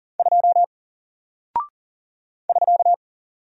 Morse code, heard as a keyed tone, about 40 wpm, 700 Hz. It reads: 2 <SK>
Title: Single Character (Letter:Number:Punctuation:Prosign) Alt 40wpm